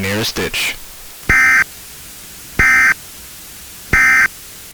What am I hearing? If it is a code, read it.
Emergency Alert System (SAME) tones, heard as harsh buzzing bursts, with speech. It is heard from 1.2 to 4.4 s.